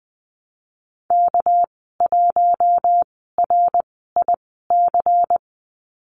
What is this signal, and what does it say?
Morse code, heard as a keyed tone, about 20 wpm, spaced as for 15 wpm, 700 Hz. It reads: K1RIC
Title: 207 Lesson 33g - Letter j - Call Signs 20wpm